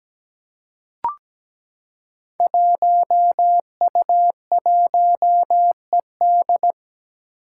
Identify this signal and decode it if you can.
Morse code, heard as a keyed tone, about 17 wpm, 700 Hz. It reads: 1U1ED
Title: Five Challenging Letter-Number Combinations 17wpm